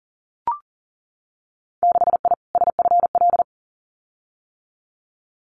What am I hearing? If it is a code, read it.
Morse code, heard as a keyed tone, about 38 wpm, 700 Hz. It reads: 6I SFL
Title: Winter Field Day Exchanges Encore 40wpm